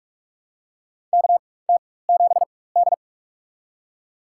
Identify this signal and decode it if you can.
Morse code, heard as a keyed tone, about 43 wpm, spaced as for 21 wpm, 690 Hz. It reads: KT7D
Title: USA Call Signs - 4x Character Spacing - 45wpm